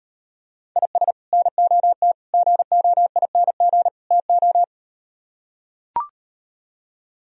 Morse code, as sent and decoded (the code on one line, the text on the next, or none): .. ... / -. --- - / --. --- .. -. --. / - ---
IS NOT GOING TO